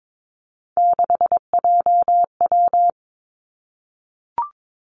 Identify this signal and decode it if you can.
Morse code, heard as a keyed tone, about 22 wpm, 700 Hz. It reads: 6JW
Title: Three Challenging Letter-Number Combinations 22wpm